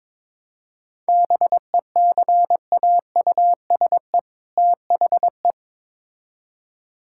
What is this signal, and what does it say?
Morse code, heard as a keyed tone, about 22 wpm, 700 Hz. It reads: BECAUSE THE
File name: Build A Phrase - 2-Word Phrases - 22wpm